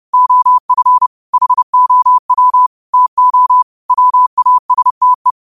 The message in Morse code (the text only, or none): OF HOW TO WASTE